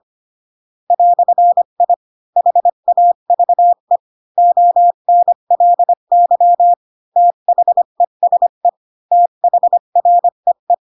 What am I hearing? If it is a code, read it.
Morse code, heard as a keyed tone, about 25 wpm, 700 Hz. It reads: "I HAVE ONLY THESE THREE
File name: Pinocchio / Pinocchio ch7 20 25 ik1hge